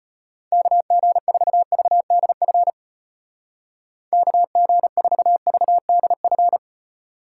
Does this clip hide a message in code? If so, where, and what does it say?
Yes: Morse code from 0.5 to 6.6 s, heard as a keyed tone, about 32 wpm, 700 Hz. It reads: KG4VDF KG4VDF